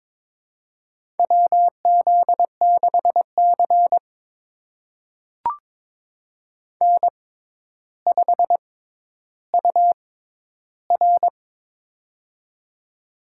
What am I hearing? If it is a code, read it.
Morse code, heard as a keyed tone, about 22 wpm, spaced as for 11 wpm, 700 Hz. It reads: WZ6C N5UR
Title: USA Call Signs - 4 Characters - 6x Character Spacing - 22wpm